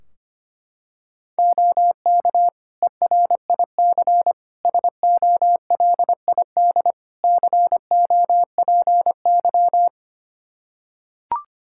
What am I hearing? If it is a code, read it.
Morse code, heard as a keyed tone, about 25 wpm, 700 Hz. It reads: OK ERIC SOLID COPY
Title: Practice QSO - Exchange 3 - More Info - 25wpm